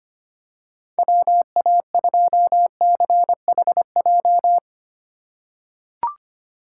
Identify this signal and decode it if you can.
Morse code, heard as a keyed tone, about 25 wpm, 700 Hz. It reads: WA2CHJ
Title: USA Call Signs - 6 Characters - 2x Character Spacing - 25wpm